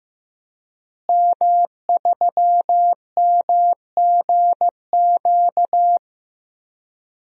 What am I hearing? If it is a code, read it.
Morse code, heard as a keyed tone, about 15 wpm, 700 Hz. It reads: M3MGQ
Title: DXCC-100 Call Signs - 5 Characters - Repeated 3x - 15wpm